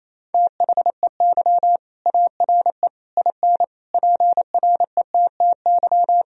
Morse code, heard as a keyed tone, about 27 wpm, 690 Hz. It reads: THEY ARE IN PRETTY
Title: Sentences from Top 1000 Words - Review Part 3 - 28wpm